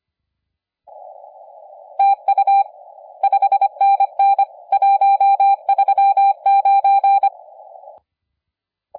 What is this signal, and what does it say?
Morse code, heard as a keyed tone, about 25 wpm, 790 Hz. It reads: TU 5NN 139
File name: la4c / sac-cw-2017 / CONTEST AUDIO 1100Z